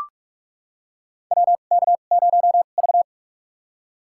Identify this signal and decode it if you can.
Morse code, heard as a keyed tone, about 43 wpm, spaced as for 32 wpm, 700 Hz. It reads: WK0V
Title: USA Call Signs - 4 Characters - 2x Character Spacing - 45wpm